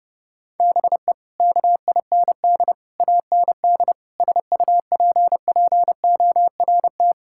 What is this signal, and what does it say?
Morse code, heard as a keyed tone, about 32 wpm, 700 Hz. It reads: BE KIND AND SUPPORT
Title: Sentences from Top 1000 Words - Review Part 5 - 30wpm